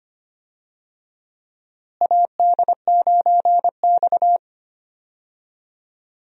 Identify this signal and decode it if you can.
Morse code, heard as a keyed tone, about 25 wpm, 700 Hz. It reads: AD9X
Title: USA Call Signs - 4 Characters - Repeated 2x - 25wpm